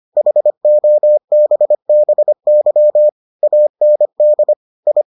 Morse code, heard as a keyed tone, about 25 wpm, 600 Hz. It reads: HOBBY AND I